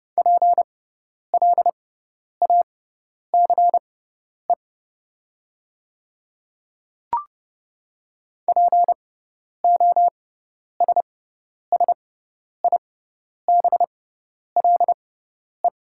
Morse code, heard as a keyed tone, about 30 wpm, spaced as for 10 wpm, 690 Hz. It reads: PLACE POSSIBLE
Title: 6x Character Spacing - Top 500 Words - 30wpm